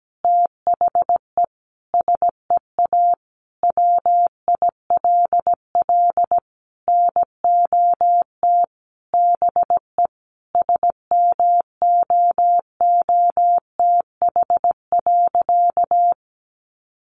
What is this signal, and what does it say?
Morse code, heard as a keyed tone, about 17 wpm, 700 Hz. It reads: THE SEA WILL NOT BE SMOOTH.
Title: Sentences from Top 1400 Words 17wpm